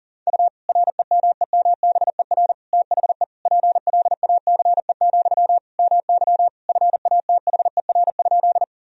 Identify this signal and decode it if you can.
Morse code, heard as a keyed tone, about 40 wpm, 700 Hz. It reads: U REMEMBER THE PLACE, MY FATHER?
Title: Sentences from Top 400 Words and Abbreviations 40wpm